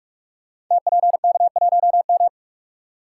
Morse code, heard as a keyed tone, about 44 wpm, 700 Hz. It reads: TPK1M
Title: Code Groups - 5 Letter:Numbers - 45wpm